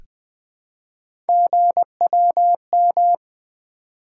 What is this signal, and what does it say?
Morse code, heard as a keyed tone, about 20 wpm, 700 Hz. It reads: GWM
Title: Three Letter-Number Combinations - 20wpm